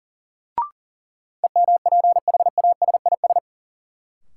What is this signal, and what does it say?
Morse code, heard as a keyed tone, about 40 wpm, 700 Hz. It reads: EMPHASIS
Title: Sentences from Top 1400 Words - New Word Drill Alt - 40wpm